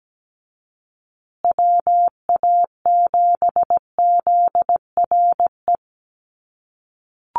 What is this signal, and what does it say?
Morse code, heard as a keyed tone, about 17 wpm, 700 Hz. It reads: WA7ZRE
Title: USA Call Signs Repeated 2x 17wpm